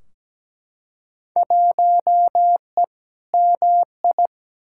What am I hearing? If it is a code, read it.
Morse code, heard as a keyed tone, about 17 wpm, 700 Hz. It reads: 1E MI
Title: Field Day Exchanges - 17wpm